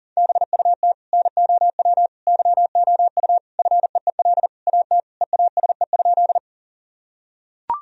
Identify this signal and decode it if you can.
Morse code, heard as a keyed tone, about 40 wpm, 700 Hz. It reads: BUT NOW YOU FEEL AT EASE?
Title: Sentences from Top 1000 Words - Review Part 1 - 40wpm